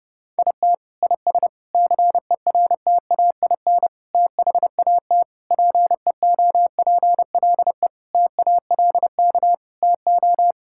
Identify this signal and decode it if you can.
Morse code, heard as a keyed tone, about 32 wpm, 700 Hz. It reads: IT IS CERTAIN THAT PEOPLE TALK TO
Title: Sentences from Top 1000 Words - Review Part 5 - 30wpm